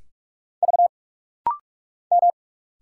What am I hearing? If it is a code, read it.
Morse code, heard as a keyed tone, about 46 wpm, 690 Hz. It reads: V M
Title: Single Letters - Rapid-Fire 45wpm